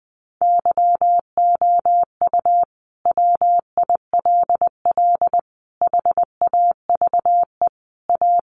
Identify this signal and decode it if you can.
Morse code, heard as a keyed tone, about 19 wpm, 700 Hz. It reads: YOU WILL HAVE A
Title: Sentences from Top 1100 Words 20wpm